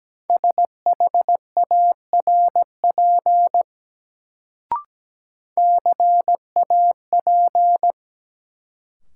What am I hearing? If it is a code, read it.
Morse code, heard as a keyed tone, about 17 wpm, 700 Hz. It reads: SHARP CAP
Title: Sentences from Top 1600 Words - New Word Drill - 17wpm